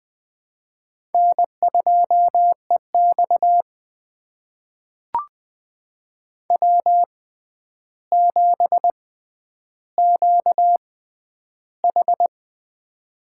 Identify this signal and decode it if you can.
Morse code, heard as a keyed tone, about 21 wpm, spaced as for 10 wpm, 700 Hz. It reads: N2EX W7QH
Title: USA Call Signs - 4 Characters - 6x Character Spacing - 20wpm